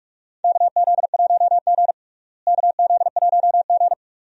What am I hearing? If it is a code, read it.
Morse code, heard as a keyed tone, about 46 wpm, 700 Hz. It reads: KZ1G KZ1G
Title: USA Call Signs - 4 Characters - Repeated 3x - 45wpm